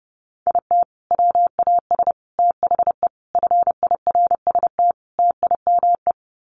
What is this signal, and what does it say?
Morse code, heard as a keyed tone, about 30 wpm, 700 Hz. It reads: IT WAS THE FIRST TIME
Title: Sets of 5 Words 30wpm